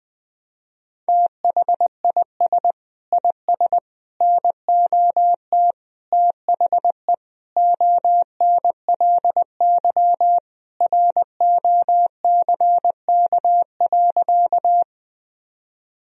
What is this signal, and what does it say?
Morse code, heard as a keyed tone, about 21 wpm, 700 Hz. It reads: THIS IS NOT THE ONLY ROCK.
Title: Sentences from Top 1000 Words - Review Part 2 - 20wpm